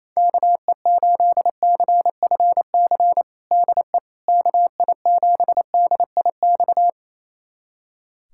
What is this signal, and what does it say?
Morse code, heard as a keyed tone, about 27 wpm, 700 Hz. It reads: KE8CFC DE KI7DIX
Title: USA Callsign DE Callsign - 6 Characters - 28wpm